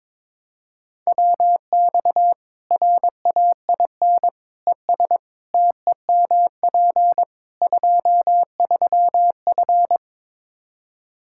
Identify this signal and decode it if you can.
Morse code, heard as a keyed tone, about 22 wpm, 700 Hz. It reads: WX RAIN ES TEMP 23F
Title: Practice QSO - Exchange 2 - Info - 22wpm